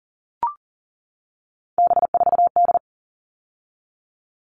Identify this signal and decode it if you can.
Morse code, heard as a keyed tone, about 40 wpm, 700 Hz. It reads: B4D
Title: All Cut Numbers - 3-Digit 40wpm